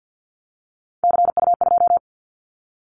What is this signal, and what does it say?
Morse code, heard as a keyed tone, about 50 wpm, 700 Hz. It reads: CU2
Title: Callsign Prefixes for Top 50 DXCC - 50wpm